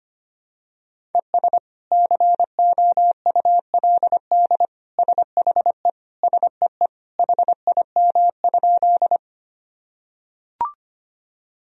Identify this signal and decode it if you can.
Morse code, heard as a keyed tone, about 25 wpm, 700 Hz. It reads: ES COULD SHE SEE HIM?
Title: Sentences from Top 100 Words and Abbreviations 25wpm